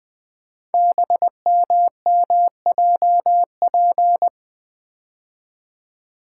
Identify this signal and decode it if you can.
Morse code, heard as a keyed tone, about 20 wpm, 700 Hz. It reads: BMMJP
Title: Code Groups - 5 Letter:Numbers - 20wpm